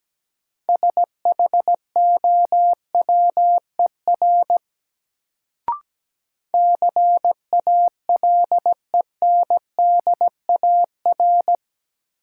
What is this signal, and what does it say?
Morse code, heard as a keyed tone, about 17 wpm, 700 Hz. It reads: SHOWER CALENDAR
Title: Sentences from Top 1600 Words - New Word Drill - 17wpm